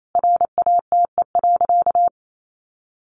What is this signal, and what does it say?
Morse code, heard as a keyed tone, about 28 wpm, 700 Hz. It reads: RATE.
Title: Sentences from Top 900 Words 28wpm